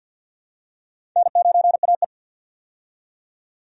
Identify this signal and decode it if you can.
Morse code, heard as a keyed tone, about 52 wpm, 700 Hz. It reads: N9AE
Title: USA Call Signs - 4 Characters - 50wpm